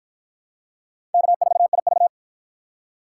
Morse code, heard as a keyed tone, about 52 wpm, 700 Hz. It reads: K4IV